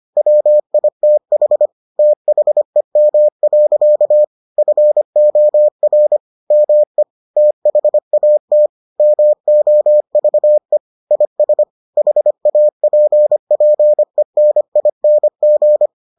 Morse code, heard as a keyed tone, about 25 wpm, 600 Hz. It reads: WITH THEM. FOR ME THAT MOVE IS HAPPENING